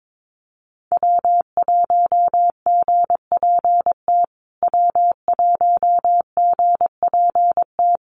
Morse code, heard as a keyed tone, about 22 wpm, 700 Hz. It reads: W1GPT W1GPT
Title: USA Call Signs Repeated 3x 22wpm